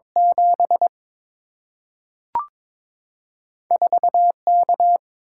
Morse code, heard as a keyed tone, about 22 wpm, 690 Hz. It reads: 7 4 K
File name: practice / 22wpm / Two Letter-Number Combinations - 22wpm